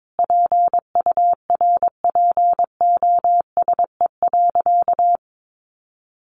Morse code, heard as a keyed tone, about 22 wpm, 700 Hz. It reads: PURPOSE.
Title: Sentences from Top 1400 Words 22wpm